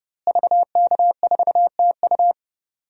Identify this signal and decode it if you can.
Morse code, heard as a keyed tone, about 30 wpm, 700 Hz. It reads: VK4TU
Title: DXCC-100 Call Signs - 5 Characters - Repeated 3x - 30wpm